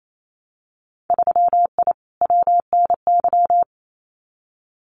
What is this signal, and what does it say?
Morse code, heard as a keyed tone, about 28 wpm, 700 Hz. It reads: 3I WNY
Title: Winter Field Day Exchanges Encore 28wpm